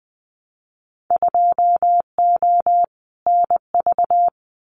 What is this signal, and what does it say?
Morse code, heard as a keyed tone, about 20 wpm, 700 Hz. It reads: 2O NV
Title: Winter Field Day Exchanges 20wpm